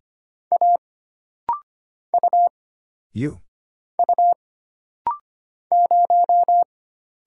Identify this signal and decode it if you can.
Morse code, heard as a keyed tone, about 25 wpm, 700 Hz. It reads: A U U 0
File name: Single Character (Letter:Number:Punctuation:Prosign) - Rapid-Fire - 25wpm